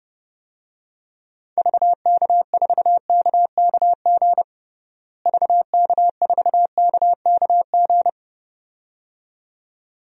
Morse code, heard as a keyed tone, about 30 wpm, 700 Hz. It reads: VK4KKG VK4KKG
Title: DXCC-100 Call Signs - 6 Characters - Repeated 2x - 30wpm